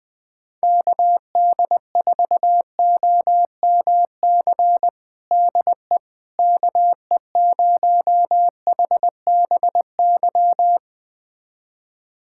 Morse code, heard as a keyed tone, about 20 wpm, 700 Hz. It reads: KD4OMC DE KE0HBY
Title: USA Callsign DE Callsign - 6 Characters - 20wpm